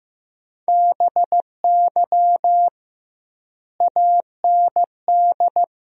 Morse code, heard as a keyed tone, about 15 wpm, 700 Hz. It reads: BY AND
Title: 2x Word-Spacing - Sentences from Top 300 Words - 15wpm